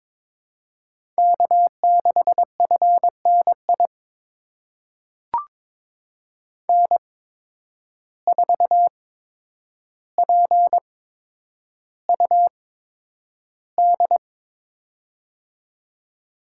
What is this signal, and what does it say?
Morse code, heard as a keyed tone, about 22 wpm, spaced as for 9 wpm, 700 Hz. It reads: K6FNI N4PUD
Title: USA Call Signs - 5 Characters - 8x Character Spacing - 22wpm